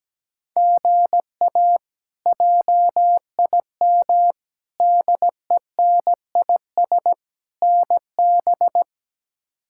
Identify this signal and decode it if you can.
Morse code, heard as a keyed tone, about 17 wpm, 700 Hz. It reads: GA JIM DENIS NB
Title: K1USN SST QSO Practice - 17wpm